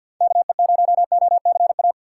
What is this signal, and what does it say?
Morse code, heard as a keyed tone, about 48 wpm, 700 Hz. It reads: KE0OCA
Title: USA Call Signs - 2x Character Spacing - 50wpm